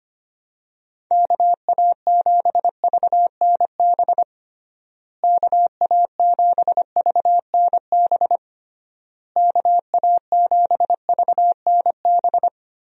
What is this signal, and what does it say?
Morse code, heard as a keyed tone, about 25 wpm, 700 Hz. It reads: KA7VNB KA7VNB KA7VNB
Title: USA Call Signs - 6 Characters - Repeated 3x - 25wpm